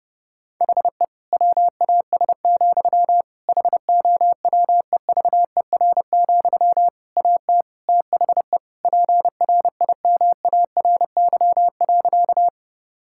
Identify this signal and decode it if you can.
Morse code, heard as a keyed tone, about 30 wpm, 700 Hz. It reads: HE WAS, HOWEVER, AT THE PRIMARY.
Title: Sentences from Top 1000 Words - Review Part 5 - 30wpm